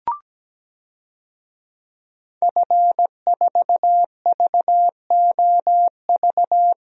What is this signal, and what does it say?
Morse code, heard as a keyed tone, about 17 wpm, 700 Hz. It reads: F4VOV
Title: DXCC-100 Call Signs - 5 Characters - 17wpm